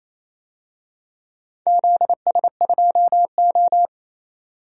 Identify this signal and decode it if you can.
Morse code, heard as a keyed tone, about 27 wpm, 700 Hz. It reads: ZS2O
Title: DXCC-100 Call Signs - 4 Characters - 28wpm